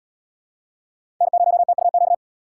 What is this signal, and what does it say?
Morse code, heard as a keyed tone, about 71 wpm, 700 Hz. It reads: N0TUQ